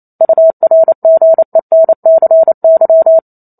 Morse code, heard as a keyed tone, about 28 wpm, 650 Hz. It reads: URGENCY
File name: Chapter 2 - 28 WPM